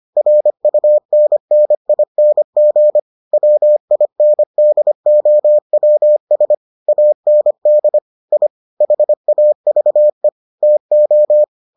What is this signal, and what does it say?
Morse code, heard as a keyed tone, about 25 wpm, 600 Hz. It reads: RUNNING WINDOWS AND I HAVE TO